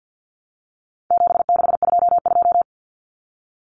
Morse code, heard as a keyed tone, about 50 wpm, 700 Hz. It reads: 7622
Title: Common Cut Numbers - 4-Digit 50wpm